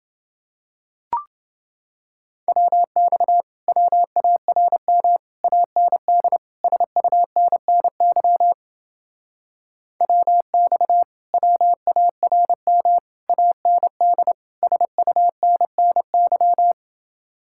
Morse code, heard as a keyed tone, about 28 wpm, 700 Hz. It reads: WX WARM AND SUNNY WX WARM AND SUNNY